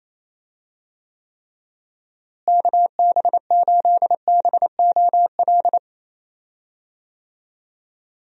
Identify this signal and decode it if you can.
Morse code, heard as a keyed tone, about 28 wpm, 700 Hz. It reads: KB8BOL
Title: USA Call Signs - 6 Characters - Repeated 2x - 28wpm